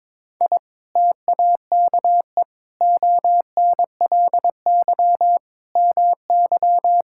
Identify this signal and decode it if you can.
Morse code, heard as a keyed tone, about 21 wpm, 700 Hz. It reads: I TAKE ONLY MY
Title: Sentences from Top 1000 Words - Review Part 5 - 22wpm